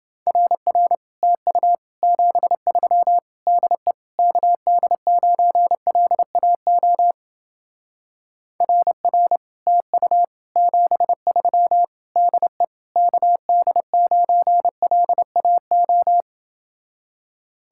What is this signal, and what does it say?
Morse code, heard as a keyed tone, about 30 wpm, 700 Hz. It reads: RR TU 73 DE KD9LAO RR TU 73 DE KD9LAO